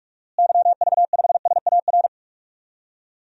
Morse code, heard as a keyed tone, about 42 wpm, 700 Hz. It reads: YV5SAR
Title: DXCC-100 Call Signs - 6 Characters - Repeated 3x - 45wpm